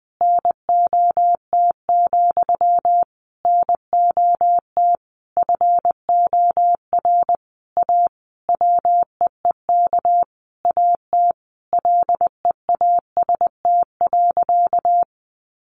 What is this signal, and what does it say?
Morse code, heard as a keyed tone, about 20 wpm, 700 Hz. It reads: NOT, NOT FOR A WEEK AT LEAST.
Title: Sentences from Top 900 Words 20wpm